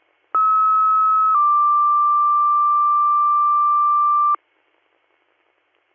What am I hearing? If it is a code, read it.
Two-tone page, heard as a one-second tone, then a three-second tone, about 1290 then 1180 Hz. It is heard from 0.3 to 4.4 s.